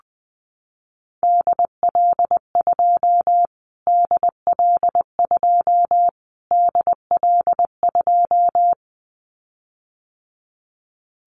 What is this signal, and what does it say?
Morse code, heard as a keyed tone, about 19 wpm, 700 Hz. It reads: DL2 DL2 DL2
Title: Callsign Prefixes for Top 25 DXCC - Repeated 3x - 20wpm